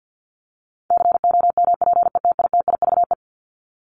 Morse code, heard as a keyed tone, about 48 wpm, 700 Hz. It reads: COMPETITIVE
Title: Top 1001-1250 Words 50wpm